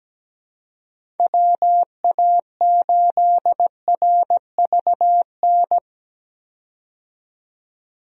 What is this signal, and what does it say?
Morse code, heard as a keyed tone, about 17 wpm, 700 Hz. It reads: WA8RVN
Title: USA Call Signs - 6 Characters - 17wpm